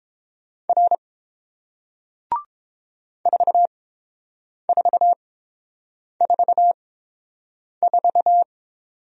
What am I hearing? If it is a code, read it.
Morse code, heard as a keyed tone, about 30 wpm, spaced as for 7 wpm, 700 Hz. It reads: R 4444